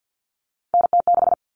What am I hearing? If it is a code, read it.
Morse code, heard as a keyed tone, about 50 wpm, 700 Hz. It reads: NT6
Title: All Cut Numbers - 3-Digit 50wpm